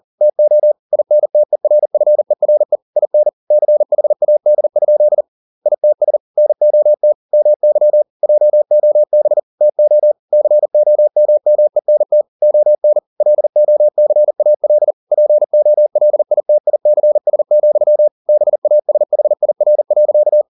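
Morse code, heard as a keyed tone, about 40 wpm, 600 Hz. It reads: TO INTERFERE IN CHAD? ITS NOT MY JOB TO COMMENT ON LOCAL POLITICS, BASHIR.
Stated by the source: Morse code at 40 wpm, practice text, CW med Gnister.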